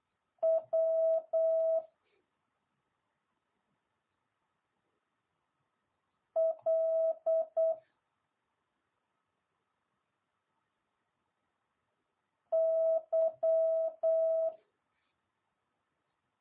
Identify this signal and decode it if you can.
Morse code, heard as a keyed tone, about 8 wpm, 650 Hz. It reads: W L Y